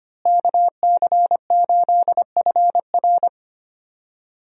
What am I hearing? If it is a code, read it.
Morse code, heard as a keyed tone, about 25 wpm, 700 Hz. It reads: KC8FR
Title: USA Call Signs - 5 Characters - Repeated 2x - 25wpm